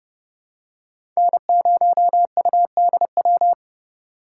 Morse code, heard as a keyed tone, about 32 wpm, 700 Hz. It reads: N0UDW
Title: USA Call Signs - 5 Characters - 4x Character Spacing - 30wpm